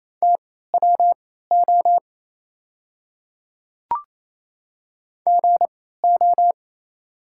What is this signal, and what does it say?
Morse code, heard as a keyed tone, about 28 wpm, spaced as for 16 wpm, 700 Hz. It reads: TWO GO
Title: 3x Character Spacing - Top 100 Words - 28wpm